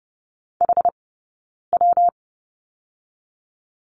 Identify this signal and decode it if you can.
Morse code, heard as a keyed tone, about 30 wpm, 700 Hz. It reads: H W
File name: Lesson 19d - Letter w - Two Characters 30wpm